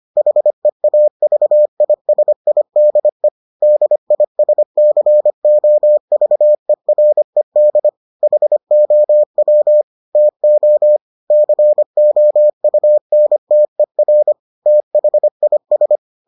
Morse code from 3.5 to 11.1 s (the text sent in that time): DISCOVERED HOW TO